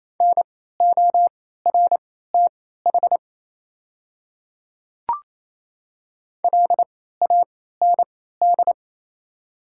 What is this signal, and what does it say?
Morse code, heard as a keyed tone, about 28 wpm, spaced as for 16 wpm, 700 Hz. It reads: NORTH LAND
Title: 3x Character Spacing - Top 500 Words - 28wpm